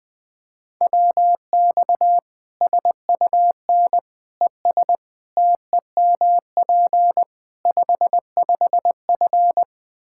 Morse code, heard as a keyed tone, about 20 wpm, 700 Hz. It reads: WX SUN ES TEMP 55F
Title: K1USN SST QSO Practice 20wpm